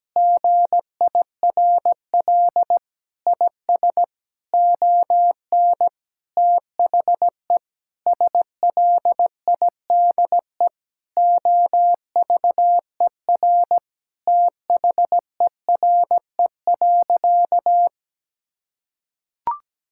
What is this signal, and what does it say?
Morse code, heard as a keyed tone, about 17 wpm, 700 Hz. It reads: GIRL IS ON THE SLIDE OVER THERE.
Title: Sentences from Top 1600 Words 17wpm